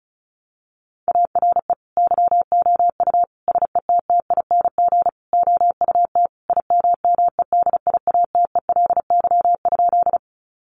Morse code, heard as a keyed tone, about 34 wpm, 700 Hz. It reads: ARE YOU SETTING OUT IMMEDIATELY?
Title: Sentences from Top 1300 Words 35wpm